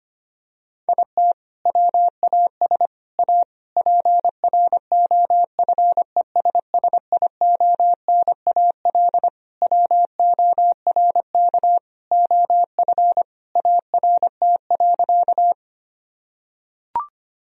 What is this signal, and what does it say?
Morse code, heard as a keyed tone, about 25 wpm, 700 Hz. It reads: IT WAS A PROFESSIONAL WORK OF ART.
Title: Sentences from Top 1000 Words - Review Part 4 - 25wpm